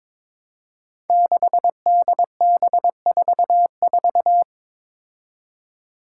Morse code, heard as a keyed tone, about 22 wpm, 700 Hz. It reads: 6DB44
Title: Five Challenging Letter-Number Combinations 22wpm